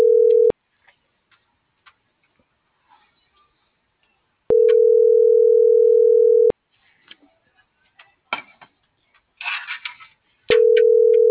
Ringback tone (North America), 440 + 480 Hz.